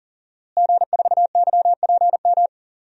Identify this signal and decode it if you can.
Morse code, heard as a keyed tone, about 40 wpm, 700 Hz. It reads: G4YPM
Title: DXCC-100 Call Signs - 5 Characters - Repeated 2x - 40wpm